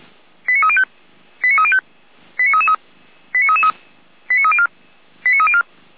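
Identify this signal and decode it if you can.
Five-tone selective call (ZVEI), heard as quick runs of five tones, tones of about 70 ms. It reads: 89305 89305 89333 89304 89304
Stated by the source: Five-tone ZVEI selective call received on an SDR.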